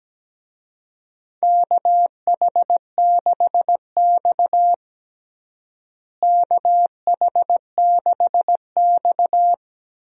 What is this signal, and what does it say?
Morse code, heard as a keyed tone, about 17 wpm, 700 Hz. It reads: KH6X KH6X
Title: DXCC-100 Call Signs - 4 Characters - Repeated 2x - 17wpm